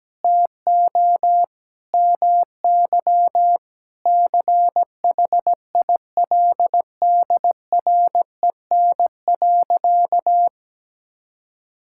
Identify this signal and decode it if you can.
Morse code, heard as a keyed tone, about 17 wpm, 700 Hz. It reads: TO MY CHILDREN.
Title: Sentences from Top 1000 Words - Review Part 5 - 17wpm